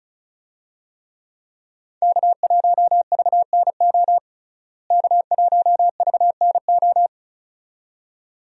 Morse code, heard as a keyed tone, about 36 wpm, 700 Hz. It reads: K1VNO K1VNO
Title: USA Call Signs - 5 Characters - Repeated 2x - 35wpm